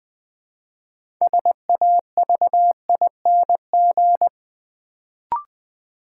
Morse code, heard as a keyed tone, about 20 wpm, 700 Hz. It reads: SAVING
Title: Build A Word - 3 Letters At A Time - 6-Letter Words - 20wpm